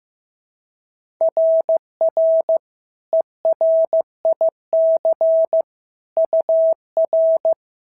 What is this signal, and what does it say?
Morse code, heard as a keyed tone, about 14 wpm, 650 Hz. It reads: RR ERIC UR
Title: SOTA QSO Practice 15wpm